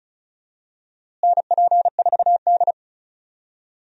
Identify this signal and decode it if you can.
Morse code, heard as a keyed tone, about 36 wpm, 700 Hz. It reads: NP4D
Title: USA Call Signs - 4 Characters - Repeated 3x - 35wpm